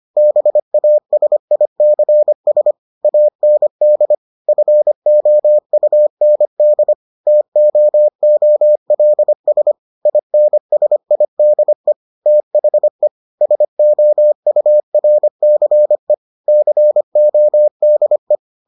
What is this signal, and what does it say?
Morse code, heard as a keyed tone, about 25 wpm, 600 Hz. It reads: BASICS AND FOUND TOOLS INSIDE THE SOURCE CODE